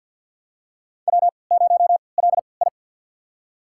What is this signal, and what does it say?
Morse code, heard as a keyed tone, about 46 wpm, spaced as for 27 wpm, 700 Hz. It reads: W0LI
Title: USA Call Signs - 3x Character Spacing - 50wpm